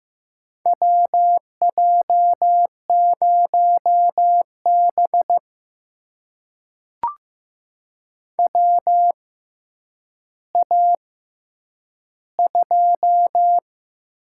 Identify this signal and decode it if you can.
Morse code, heard as a keyed tone, about 15 wpm, spaced as for 9 wpm, 700 Hz. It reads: WJ0B WA2
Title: USA Call Signs - 6x Character Spacing - 15wpm